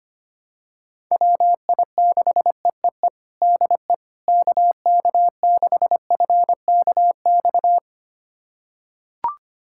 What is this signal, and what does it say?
Morse code, heard as a keyed tone, about 25 wpm, 700 Hz. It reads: WI6EEE DE KK6FKX
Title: USA Callsign DE Callsign - 6 Characters - 25wpm